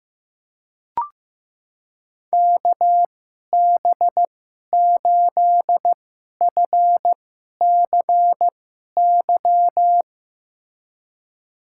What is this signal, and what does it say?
Morse code, heard as a keyed tone, about 15 wpm, spaced as for 11 wpm, 700 Hz. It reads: KB8FCY
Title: USA Call Signs - 6 Characters - 2x Character Spacing - 15wpm